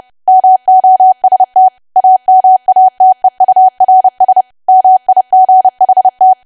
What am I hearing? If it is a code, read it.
Morse code, heard as a keyed tone, about 30 wpm, 750 Hz. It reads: MOST AMATEURS MIGHT
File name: cwa / W1AW-Level3 / 30WPM